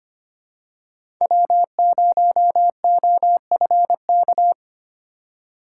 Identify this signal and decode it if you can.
Morse code, heard as a keyed tone, about 25 wpm, 700 Hz. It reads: W0OFK
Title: USA Call Signs - 6x Character Spacing - 25wpm